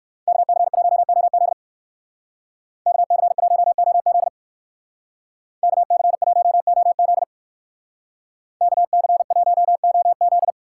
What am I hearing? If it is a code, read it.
Morse code, heard as a keyed tone, about 54 wpm, 690 Hz. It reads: KC1OZ KC1OZ KC1OZ KC1OZ
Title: Speed Racing - USA 2x2 Call Signs 45wpm